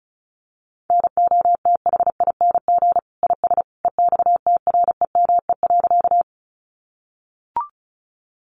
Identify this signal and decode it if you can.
Morse code, heard as a keyed tone, about 34 wpm, 700 Hz. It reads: NOTHING IS EXTREME.
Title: Sentences from Top 1300 Words 35wpm